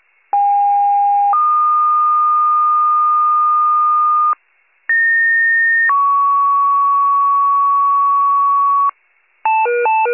Two-tone page, heard as a one-second tone, then a three-second tone, about 810 then 1220 Hz, about 1850 then 1120 Hz. It is heard from 0.3 to 8.9 s.